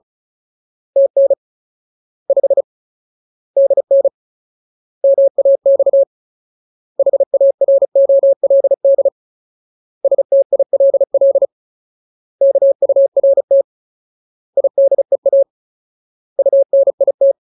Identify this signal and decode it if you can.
Morse code, heard as a keyed tone, about 34 wpm, 550 Hz. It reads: TN 5 DN MAX HAROLD STILL KURT IDEA UNIT
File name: Ultimate Word List / 35WPM Ultimate